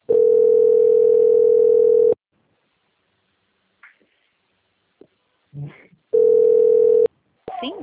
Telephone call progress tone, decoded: ringback tone

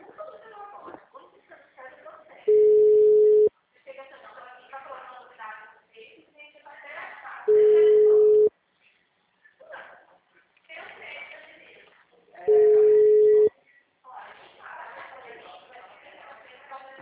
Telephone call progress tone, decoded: ringback tone

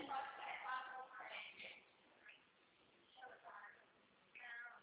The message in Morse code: none

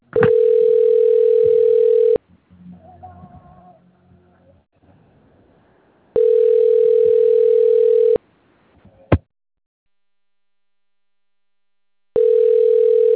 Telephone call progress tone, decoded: ringback tone

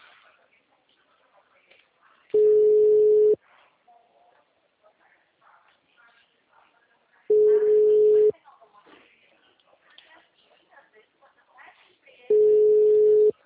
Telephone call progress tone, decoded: ringback tone

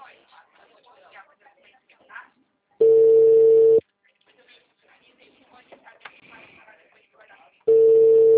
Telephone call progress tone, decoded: ringback tone

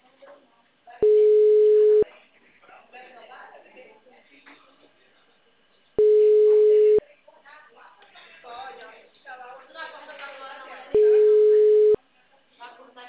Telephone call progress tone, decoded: ringback tone

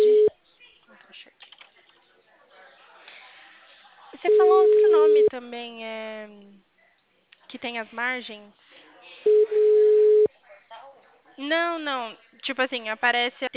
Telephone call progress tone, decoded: ringback tone

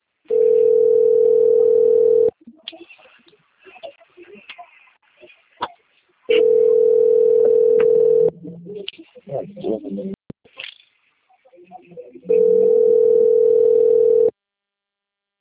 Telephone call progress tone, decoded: ringback tone

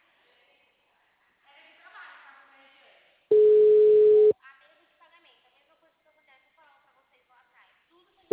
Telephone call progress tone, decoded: ringback tone

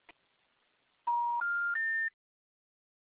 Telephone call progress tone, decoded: special information tone

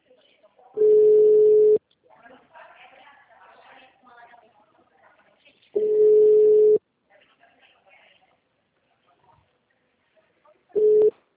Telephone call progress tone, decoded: ringback tone

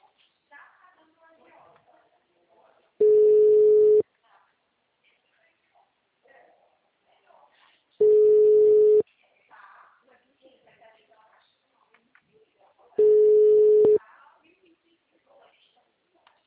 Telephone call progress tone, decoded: ringback tone

